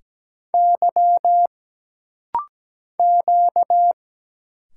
Morse code, heard as a keyed tone, about 17 wpm, 700 Hz. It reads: Y Q
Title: Single Letters - Rapid-Fire 17wpm